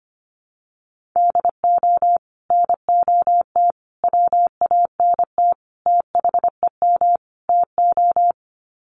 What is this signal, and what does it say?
Morse code, heard as a keyed tone, about 25 wpm, 700 Hz. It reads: DO NOT WANT THEM TO
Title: Sets of 5 Words Encore 25wpm